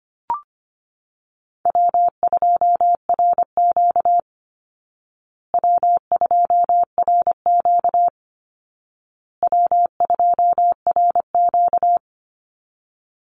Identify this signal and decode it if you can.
Morse code, heard as a keyed tone, about 25 wpm, 700 Hz. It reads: W2RQ W2RQ W2RQ
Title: Call Signs Repeated 2x - 25wpm